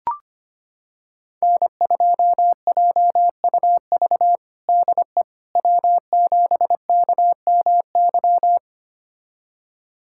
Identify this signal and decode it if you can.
Morse code, heard as a keyed tone, about 25 wpm, 700 Hz. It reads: N2JUV DE W7KMY
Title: USA Callsign DE Callsign - 5 Characters - 25wpm